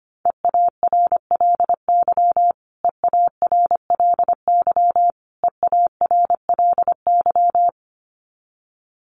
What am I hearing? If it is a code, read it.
Morse code, heard as a keyed tone, about 25 wpm, 700 Hz. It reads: EARLY EARLY EARLY
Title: Top 500 Words Repeated 25wpm